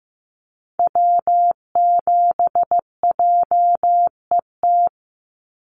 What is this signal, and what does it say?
Morse code, heard as a keyed tone, about 15 wpm, 700 Hz. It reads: W7JET
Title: NA SOTA Activator Callsigns 15wpm